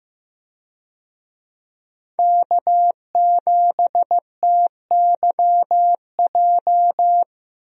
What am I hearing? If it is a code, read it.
Morse code, heard as a keyed tone, about 16 wpm, 700 Hz. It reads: K7TYJ
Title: USA Call Signs - 5 Characters - Repeated 2x - 15wpm